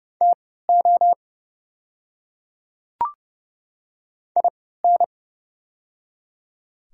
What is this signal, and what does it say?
Morse code, heard as a keyed tone, about 30 wpm, spaced as for 17 wpm, 690 Hz. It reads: TO IN
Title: 3x Character Spacing - Top 100 Words - 30wpm